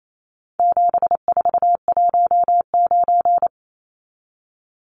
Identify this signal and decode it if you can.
Morse code, heard as a keyed tone, about 28 wpm, 700 Hz. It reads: 7419
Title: Numbers - 4-Digit 28wpm